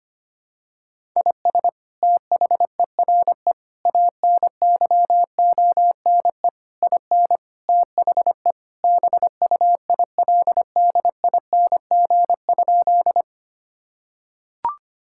Morse code, heard as a keyed tone, about 25 wpm, 700 Hz. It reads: IS THERE ANYONE IN THE BUILDING?
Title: Sentences from Top 1000 Words - Review Part 5 - 25wpm